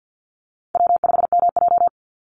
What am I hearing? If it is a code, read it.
Morse code, heard as a keyed tone, about 48 wpm, 700 Hz. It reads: W5MJ
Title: Easy Call Signs Repeated 2x - 50wpm